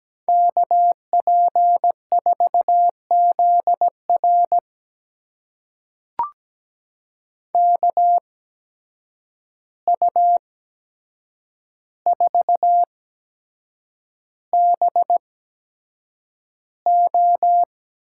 Morse code, heard as a keyed tone, about 17 wpm, spaced as for 7 wpm, 700 Hz. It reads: KP4ZR KU4BO
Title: USA Call Signs - 5 Characters - 8x Character Spacing - 17wpm